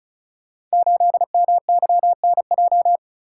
Morse code, heard as a keyed tone, about 36 wpm, 700 Hz. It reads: 8MYNJ
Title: Code Groups - 5 Letter:Numbers - 35wpm